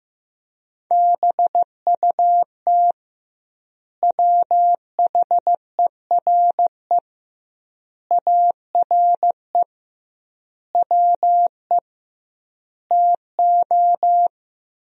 Morse code, heard as a keyed tone, about 15 wpm, 700 Hz. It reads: BUT WHERE ARE WE TO
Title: 2x Word-Spacing - Sentences from Top 200 Words - 15wpm